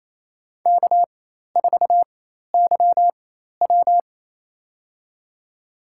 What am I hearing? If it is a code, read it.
Morse code, heard as a keyed tone, about 27 wpm, spaced as for 13 wpm, 700 Hz. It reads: K4YW
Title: USA Call Signs - 4 Characters - 4x Character Spacing - 28wpm